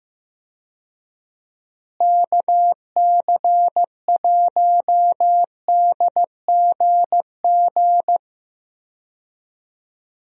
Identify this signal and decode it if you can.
Morse code, heard as a keyed tone, about 15 wpm, 700 Hz. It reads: KC1DGG
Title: USA Call Signs - 6 Characters - Repeated 2x - 15wpm